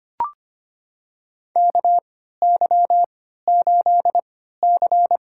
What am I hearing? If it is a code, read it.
Morse code, heard as a keyed tone, about 25 wpm, spaced as for 14 wpm, 700 Hz. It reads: KY8C
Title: USA Call Signs - 4 Characters - 3x Character Spacing - 25wpm